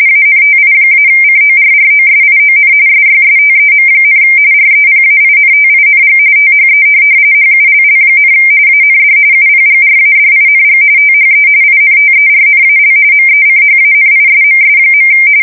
RTTY, about 45.45 baud. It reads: ANTI-CORROSION SYSTEM, REPAINTING THE HULL, AND UPGRADING THE INTERNAL MECHANISMS.
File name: fsk170 / USS-Missouri